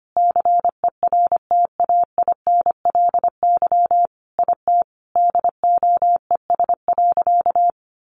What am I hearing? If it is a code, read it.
Morse code, heard as a keyed tone, about 25 wpm, 700 Hz. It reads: CERTAINLY IT DOES.
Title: Sentences from Top 900 Words 25wpm